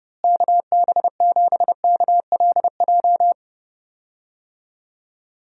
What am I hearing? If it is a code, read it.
Morse code, heard as a keyed tone, about 32 wpm, 700 Hz. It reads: KB7KLJ
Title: USA Call Signs - 6 Characters - Repeated 2x - 30wpm